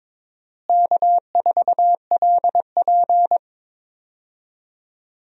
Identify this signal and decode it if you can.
Morse code, heard as a keyed tone, about 22 wpm, 700 Hz. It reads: K4LP
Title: USA Call Signs - 4 Characters - Repeated 2x - 22wpm